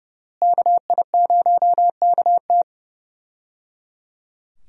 Morse code, heard as a keyed tone, about 32 wpm, 700 Hz. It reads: KI0KT
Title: USA Call Signs - 5 Characters - Repeated 3x - 30wpm